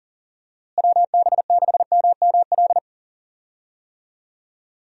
Morse code, heard as a keyed tone, about 40 wpm, 700 Hz. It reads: WB6MML
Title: USA Call Signs - 6 Characters - Repeated 3x - 40wpm